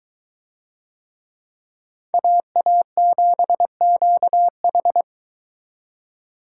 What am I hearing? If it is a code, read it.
Morse code, heard as a keyed tone, about 22 wpm, 700 Hz. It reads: AA7QH